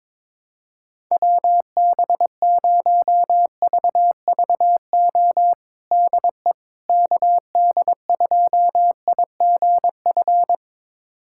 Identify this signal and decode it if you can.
Morse code, heard as a keyed tone, about 22 wpm, 700 Hz. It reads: WB0VVO DE KD2IGF
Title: USA Callsign DE Callsign - 6 Characters - 22wpm